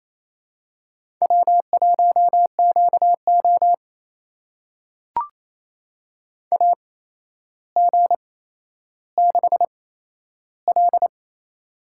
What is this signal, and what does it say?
Morse code, heard as a keyed tone, about 28 wpm, spaced as for 12 wpm, 700 Hz. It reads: W1QO AG6L
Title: USA Call Signs - 4 Characters - 8x Character Spacing - 28wpm